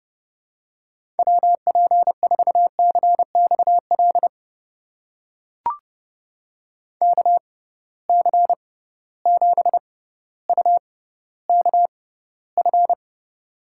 Morse code, heard as a keyed tone, about 31 wpm, spaced as for 15 wpm, 700 Hz. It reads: WP4CXL KC7UKF
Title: USA Call Signs - 6 Characters - 6x Character Spacing - 30wpm